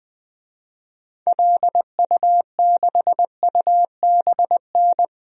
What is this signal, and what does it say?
Morse code, heard as a keyed tone, about 20 wpm, 700 Hz. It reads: LU6UBN
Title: DXCC-100 Call Signs - 6 Characters - 20wpm